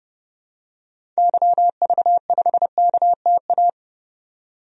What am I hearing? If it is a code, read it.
Morse code, heard as a keyed tone, about 30 wpm, 700 Hz. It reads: YV5KTA
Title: DXCC-100 Call Signs - 6 Characters - 30wpm